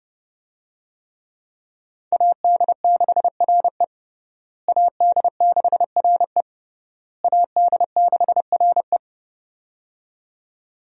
Morse code, heard as a keyed tone, about 30 wpm, 700 Hz. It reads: AD6RE AD6RE AD6RE
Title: USA Call Signs - 5 Characters - Repeated 3x - 30wpm